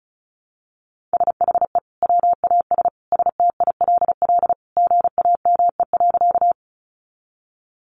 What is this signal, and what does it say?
Morse code, heard as a keyed tone, about 34 wpm, 700 Hz. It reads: SHE WAS STILL GAME.
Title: Sentences from Top 500 Words 35wpm